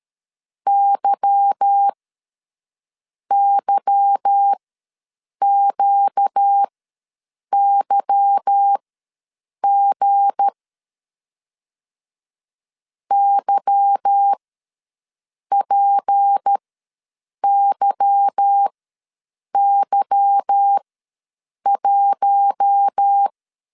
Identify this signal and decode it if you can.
Morse code, heard as a keyed tone, about 13 wpm, spaced as for 7 wpm, 800 Hz. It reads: YYQYG YPYY1
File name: LEZIONI / 8 YJ45 / 8 Y ws 12wpm 3409